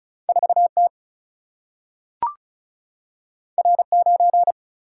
Morse code, heard as a keyed tone, about 36 wpm, 700 Hz. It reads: 4T R9
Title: Single-Letter Single-Number 35wpm